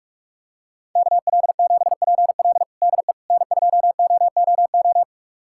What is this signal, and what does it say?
Morse code, heard as a keyed tone, about 43 wpm, 700 Hz. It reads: KL7PL DE N2OOO
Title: USA Callsign DE Callsign - 5 Characters - 45wpm